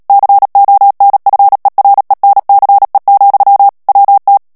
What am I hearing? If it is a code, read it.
Morse code, heard as a keyed tone, about 37 wpm, 800 Hz. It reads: CONFERENCE, WT
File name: morse mp3 / ANGLAIS texte clair 3 vitesse 35 mots minute